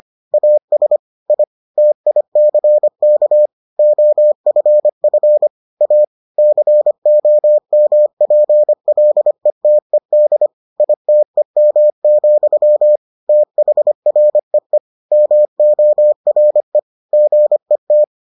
Morse code, heard as a keyed tone, about 25 wpm, 600 Hz. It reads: AS I TICK OFF A COMPLETED ITEM, THREE MORE GET